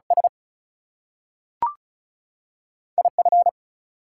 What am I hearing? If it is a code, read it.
Morse code, heard as a keyed tone, about 36 wpm, 700 Hz. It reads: S IF
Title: Two Letter Words - 35wpm